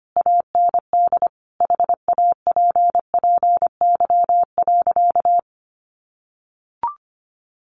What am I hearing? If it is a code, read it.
Morse code, heard as a keyed tone, about 25 wpm, 700 Hz. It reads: AND HAPPY.
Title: Sentences from Top 1000 Words 25wpm